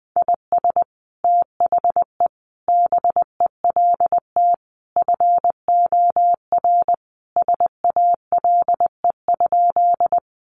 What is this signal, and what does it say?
Morse code, heard as a keyed tone, about 20 wpm, 700 Hz. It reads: IS THE BELT FOR SALE?
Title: Sentences from Top 1400 Words 20wpm